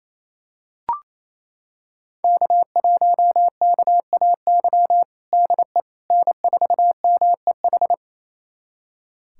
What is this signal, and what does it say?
Morse code, heard as a keyed tone, about 28 wpm, 700 Hz. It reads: K1KAY DE N4MEH
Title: USA Callsign DE Callsign - 5 Characters - 28wpm